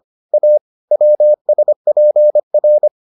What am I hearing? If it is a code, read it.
Morse code, heard as a keyed tone, about 25 wpm, 600 Hz. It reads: A WSPR